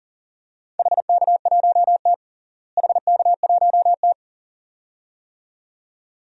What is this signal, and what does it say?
Morse code, heard as a keyed tone, about 37 wpm, 700 Hz. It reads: HK1T HK1T
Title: DXCC-100 Call Signs - 4 Characters - Repeated 2x - 40wpm